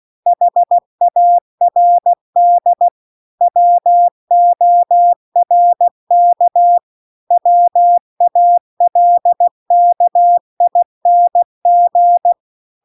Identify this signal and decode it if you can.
Morse code, heard as a keyed tone, about 16 wpm, 700 Hz. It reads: HARD WORK WALKING